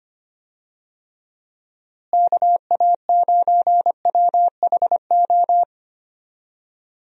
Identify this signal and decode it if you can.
Morse code, heard as a keyed tone, about 25 wpm, 700 Hz. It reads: KA9WHO
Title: USA Call Signs - 6 Characters - Repeated 2x - 25wpm